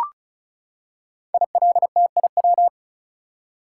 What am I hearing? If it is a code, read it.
Morse code, heard as a keyed tone, about 36 wpm, 700 Hz. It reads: ILTIW
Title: Code Groups - 5 Letters - 35wpm